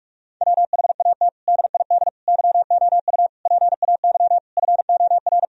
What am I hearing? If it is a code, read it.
Morse code, heard as a keyed tone, about 46 wpm, 700 Hz. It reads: WHAT DID YOU PAY FOR
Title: Sentences from Top 1000 Words - Review Part 3 - 45wpm